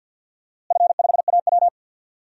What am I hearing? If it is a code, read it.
Morse code, heard as a keyed tone, about 51 wpm, 700 Hz. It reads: F5AW
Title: DXCC-100 Call Signs - 4 Characters - Repeated 3x - 50wpm